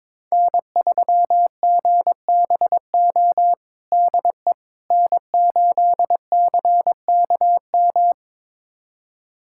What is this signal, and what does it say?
Morse code, heard as a keyed tone, about 22 wpm, 700 Hz. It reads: N3GBO DE N8CKM
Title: USA Callsign DE Callsign - 5 Characters - 22wpm